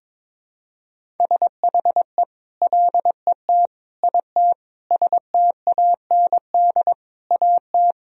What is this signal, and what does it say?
Morse code, heard as a keyed tone, about 22 wpm, 700 Hz. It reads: SHE LET IT STAND AT
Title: Sentences from Top 1000 Words - Review Part 1 - 22wpm